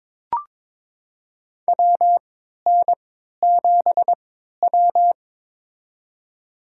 Morse code, heard as a keyed tone, about 22 wpm, spaced as for 13 wpm, 700 Hz. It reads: WN7W
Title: USA Call Signs - 4 Characters - 3x Character Spacing - 22wpm